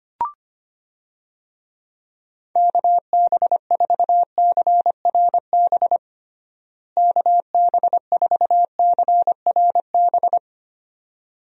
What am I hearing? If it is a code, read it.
Morse code, heard as a keyed tone, about 25 wpm, 700 Hz. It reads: KB4CRB KB4CRB
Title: USA Call Signs - 6 Characters - Repeated 2x - 25wpm